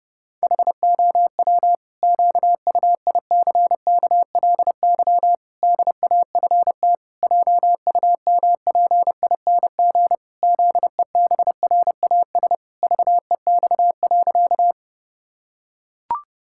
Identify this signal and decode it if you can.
Morse code, heard as a keyed tone, about 30 wpm, 700 Hz. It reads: HOW QUICKLY DAFT JUMPING ZEBRAS VEX.
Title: Pangram Challenge v2 - 30wpm